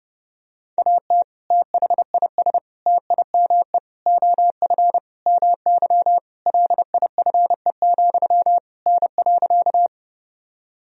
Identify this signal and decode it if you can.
Morse code, heard as a keyed tone, about 30 wpm, 700 Hz. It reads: AT THIS TIME OF MY LIFE, N.
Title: Sentences from Top 300 Words and Abbreviations 30wpm